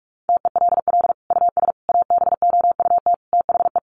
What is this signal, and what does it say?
Morse code, heard as a keyed tone, about 44 wpm, 700 Hz. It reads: TELL US ABOUT THE
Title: Sentences from Top 1300 Words 45wpm